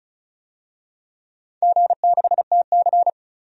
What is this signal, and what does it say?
Morse code, heard as a keyed tone, about 36 wpm, 700 Hz. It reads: G6TC
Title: DXCC-100 Call Signs - 4 Characters - 35wpm